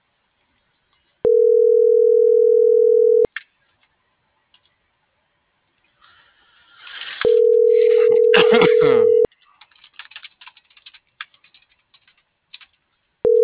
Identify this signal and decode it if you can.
Ringback tone (North America), 440 + 480 Hz.